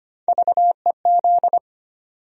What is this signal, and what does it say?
Morse code, heard as a keyed tone, about 25 wpm, 700 Hz. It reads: VEZ
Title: USA Call Sign Postfixes - 3 Letters 25wpm